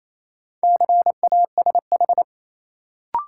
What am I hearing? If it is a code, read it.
Morse code, heard as a keyed tone, about 28 wpm, 700 Hz. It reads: CASH
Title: Sentences from Top 700 Words - New Word Drill - 28wpm